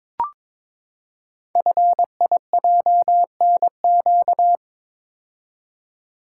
Morse code, heard as a keyed tone, about 22 wpm, 700 Hz. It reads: FIJNQ
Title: Code Groups - 5 Letters - 22wpm